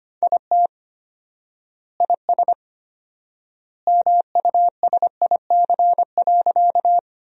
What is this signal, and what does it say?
Morse code, heard as a keyed tone, about 25 wpm, 700 Hz. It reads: IT IS MUSIC.
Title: 4x Word-Spacing - Sentences from Top 300 Words - 25wpm